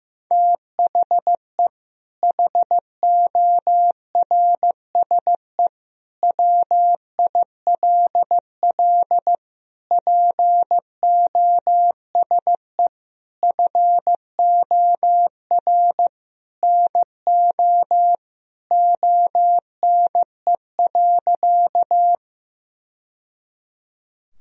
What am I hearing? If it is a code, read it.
Morse code, heard as a keyed tone, about 15 wpm, 700 Hz. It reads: THE HORSE WILL POSE FOR NO ONE.
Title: Sentences from Top 1000 Words - Review Part 2 - 15wpm